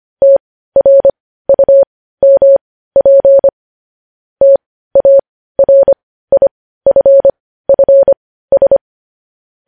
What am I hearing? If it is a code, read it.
Morse code, heard as a keyed tone, about 25 wpm, spaced as for 15 wpm, 550 Hz. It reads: TRUMP TARIFFS
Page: news headlines in morse code at 15 wpm